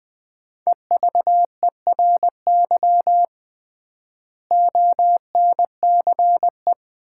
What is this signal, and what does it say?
Morse code, heard as a keyed tone, about 20 wpm, 700 Hz. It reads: EVERY ONCE
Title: 3x Word-Spacing - Sets of 5 Words - 20wpm